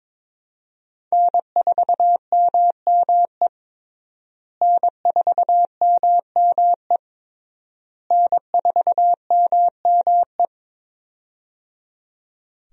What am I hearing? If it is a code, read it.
Morse code, heard as a keyed tone, about 22 wpm, 690 Hz. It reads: N4MME N4MME N4MME
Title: USA Call Signs - 5 Characters - Repeated 3x - 22wpm